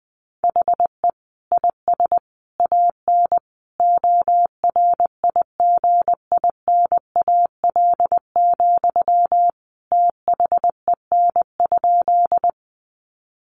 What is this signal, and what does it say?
Morse code, heard as a keyed tone, about 19 wpm, 700 Hz. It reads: HE IS AN ORIGINAL, THEN?
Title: Sentences from Top 900 Words 20wpm